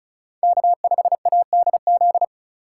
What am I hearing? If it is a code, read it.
Morse code, heard as a keyed tone, about 34 wpm, 700 Hz. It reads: K5ADZ
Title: USA Call Signs - 5 Characters - 3x Character Spacing - 35wpm